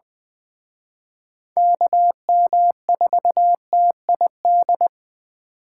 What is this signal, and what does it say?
Morse code, heard as a keyed tone, about 20 wpm, 700 Hz. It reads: KM4TID
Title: Speed Racing - USA 2x3 Call Signs 17wpm